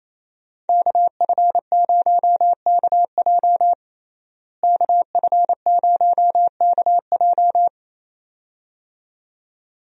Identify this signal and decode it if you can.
Morse code, heard as a keyed tone, about 28 wpm, 700 Hz. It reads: KF0KJ KF0KJ
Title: USA Call Signs - 5 Characters - Repeated 2x - 28wpm